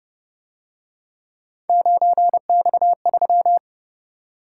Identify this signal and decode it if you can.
Morse code, heard as a keyed tone, about 29 wpm, 700 Hz. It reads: 9X3
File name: Speed-Racing - Three Letter-Number Combinations - 25wpm